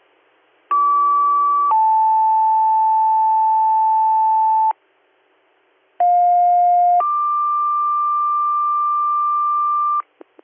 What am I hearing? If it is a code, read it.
Two-tone page, heard as a one-second tone, then a three-second tone, about 1150 then 880 Hz, about 710 then 1180 Hz. It is heard from 0.7 to 10.0 s.